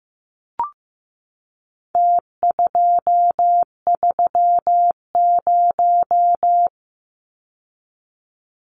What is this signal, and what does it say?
Morse code, heard as a keyed tone, about 15 wpm, 700 Hz. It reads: T230
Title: Numbers - Contest Style - 4-Digit 15wpm